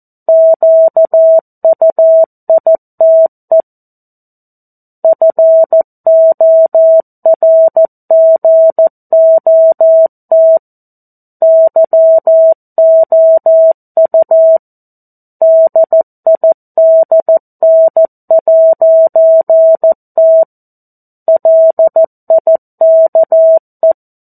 Morse code, heard as a keyed tone, about 14 wpm, 650 Hz. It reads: QUITE FORGOT YOU DIDN'T LIKE
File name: 02-Chapter-02 14 WPM